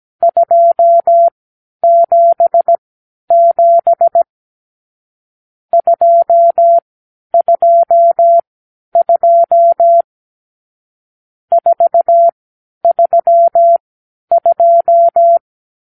Morse code, heard as a keyed tone, about 17 wpm, spaced as for 10 wpm, 680 Hz. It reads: 277 222 432
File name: k7qo / Lesson 52 (8)